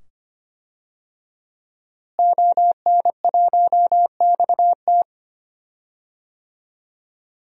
Morse code, heard as a keyed tone, about 25 wpm, 700 Hz. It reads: ON1XT
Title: DXCC-100 Call Signs - 5 Characters - Repeated 2x - 25wpm